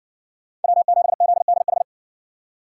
Morse code, heard as a keyed tone, about 60 wpm, 700 Hz. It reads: W7ZDH